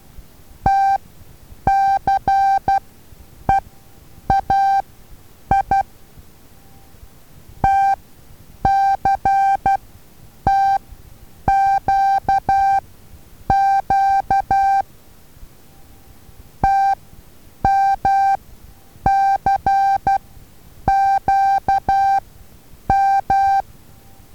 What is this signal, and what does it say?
Morse code, heard as a keyed tone, about 12 wpm, spaced as for 8 wpm, 780 Hz. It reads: TCEAI TCTQQ TMCQM
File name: Taviro HA6VA A